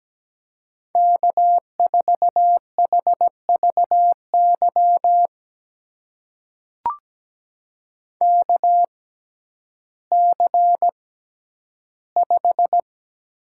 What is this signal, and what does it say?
Morse code, heard as a keyed tone, about 17 wpm, spaced as for 10 wpm, 700 Hz. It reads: K4HVY KC5
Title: USA Call Signs - 6x Character Spacing - 17wpm